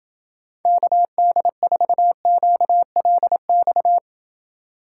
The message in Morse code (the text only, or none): KD4QLX